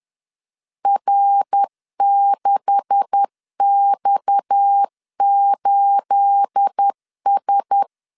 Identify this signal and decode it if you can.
Morse code, heard as a keyed tone, about 10 wpm, 800 Hz. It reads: R6X8S
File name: CW / DET / 1 9 1500C S A 10wpm 4511